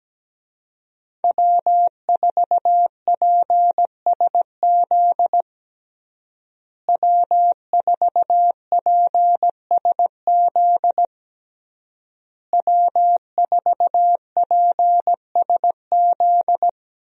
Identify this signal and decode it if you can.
Morse code, heard as a keyed tone, about 17 wpm, 700 Hz. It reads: W4PSZ W4PSZ W4PSZ
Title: USA Call Signs - 5 Characters - Repeated 3x - 17wpm